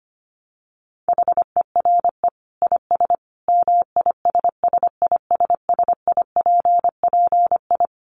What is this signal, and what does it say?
Morse code, heard as a keyed tone, about 25 wpm, 700 Hz. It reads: HERE IS MISSISSIPPI
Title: Dit Challenge 25wpm